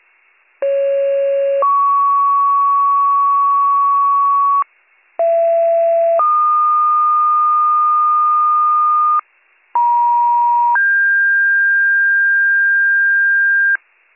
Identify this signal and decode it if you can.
Two-tone page, heard as a one-second tone, then a three-second tone, about 550 then 1120 Hz, about 670 then 1220 Hz, about 950 then 1600 Hz. It is heard from 0.6 to 13.8 s.